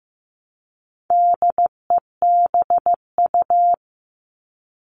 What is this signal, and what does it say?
Morse code, heard as a keyed tone, about 15 wpm, 700 Hz. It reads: DEBU
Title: All Cut Numbers - 4-Digit 15wpm